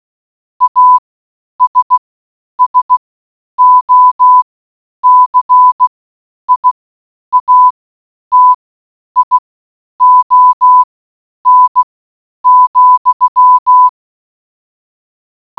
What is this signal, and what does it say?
Morse code, heard as a keyed tone, about 16 wpm, spaced as for 9 wpm, 1000 Hz. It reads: ASSOCIATION,